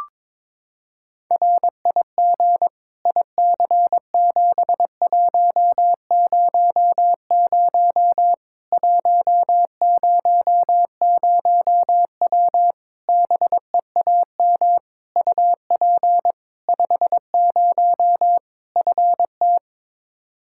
Morse code, heard as a keyed tone, about 22 wpm, 700 Hz. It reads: RIG IC7100 100W BEAM UP 50 FT
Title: Practice QSO - Exchange 2 - Info - 22wpm